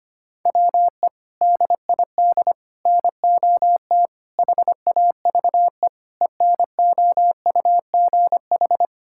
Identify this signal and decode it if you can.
Morse code, heard as a keyed tone, about 25 wpm, 700 Hz. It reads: WE DID NOT HAVE ENOUGH
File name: practice / 25wpm / Sentences with 4-Word Phrases - 25wpm